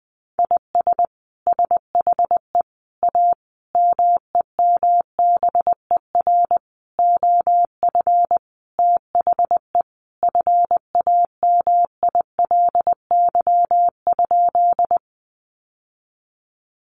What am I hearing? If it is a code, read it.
Morse code, heard as a keyed tone, about 20 wpm, 700 Hz. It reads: IS SHE A MEMBER OF THE FAMILY?
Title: Sentences from Top 1100 Words 20wpm